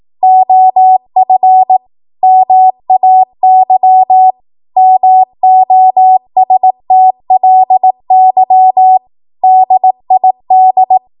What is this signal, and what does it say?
Morse code, heard as a keyed tone, about 18 wpm, 750 Hz. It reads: OF MAY MOSTLY DID
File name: Morse / 18WPM